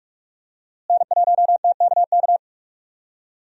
Morse code, heard as a keyed tone, about 45 wpm, 700 Hz. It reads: N1TKK
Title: USA Call Signs - 5 Characters - Repeated 3x - 45wpm